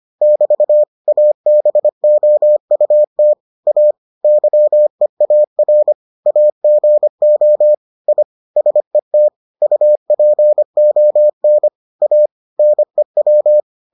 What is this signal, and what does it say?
Morse code, heard as a keyed tone, about 25 wpm, 600 Hz. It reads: <BT> ABOUT A YEAR AGO I SET UPON A NEW